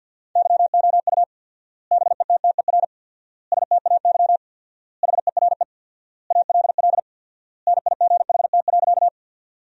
Morse code, heard as a keyed tone, about 49 wpm, 700 Hz. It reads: YOU BETTER STAY HERE ALL NIGHT.
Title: 4x Word-Spacing - Sentences from Top 500 Words - 50wpm